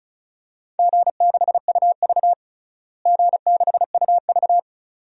Morse code, heard as a keyed tone, about 34 wpm, 700 Hz. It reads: G6UV G6UV
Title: DXCC-100 Call Signs - 4 Characters - Repeated 2x - 35wpm